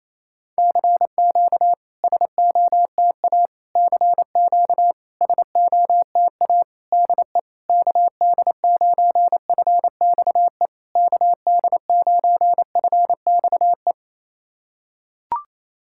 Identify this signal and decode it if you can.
Morse code, heard as a keyed tone, about 28 wpm, 690 Hz. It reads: CQ SOTA CQ SOTA DE KD9FXE KD9FXE
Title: SOTA QSO Practice 28wpm